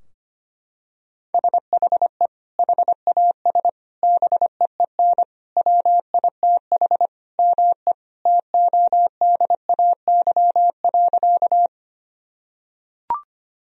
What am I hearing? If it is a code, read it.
Morse code, heard as a keyed tone, about 25 wpm, 700 Hz. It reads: SHE HAS BEEN WITH ME TODAY.
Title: Sentences from Top 1000 Words - Review Part 4 - 25wpm